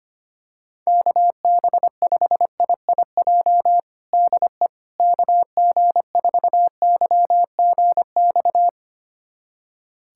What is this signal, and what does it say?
Morse code, heard as a keyed tone, about 25 wpm, 700 Hz. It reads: KB5IIJ DE KG4YGX
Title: USA Callsign DE Callsign - 6 Characters - 25wpm